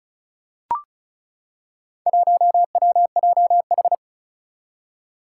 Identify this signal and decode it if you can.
Morse code, heard as a keyed tone, about 36 wpm, 700 Hz. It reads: 1WJH
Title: Four Challenging Letter-Number Combinations 35wpm